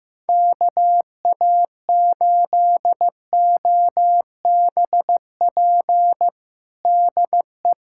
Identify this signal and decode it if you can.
Morse code, heard as a keyed tone, about 15 wpm, 700 Hz. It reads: KA8OBP DE
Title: USA Callsign DE Callsign - 6 Characters - 15wpm